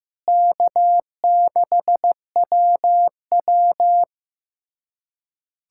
Morse code, heard as a keyed tone, about 16 wpm, 700 Hz. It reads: K6WW
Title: USA Callsign DE Callsign - 4 Characters - 15wpm